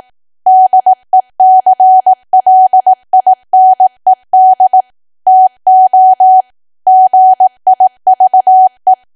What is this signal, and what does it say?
Morse code, heard as a keyed tone, about 18 wpm, 750 Hz. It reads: DECLINED TO GIVE
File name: cwa / W1AW-Level2 / 18WPM